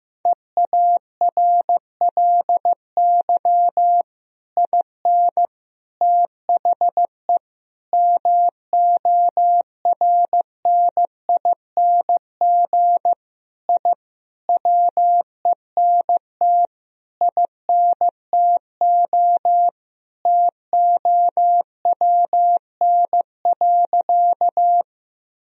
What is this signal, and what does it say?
Morse code, heard as a keyed tone, about 16 wpm, 700 Hz. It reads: EARLY IN THE MORNING I WENT INTO TOWN.
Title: Sentences from Top 1000 Words - Review Part 2 - 15wpm